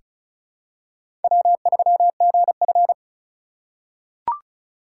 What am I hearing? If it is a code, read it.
Morse code, heard as a keyed tone, about 34 wpm, 700 Hz. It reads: W3GF
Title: USA Call Signs - 4 Characters - 6x Character Spacing - 35wpm